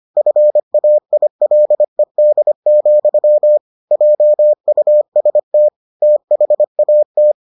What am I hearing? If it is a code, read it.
Morse code, heard as a keyed tone, about 25 wpm, 600 Hz. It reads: FAILED, JUST THAT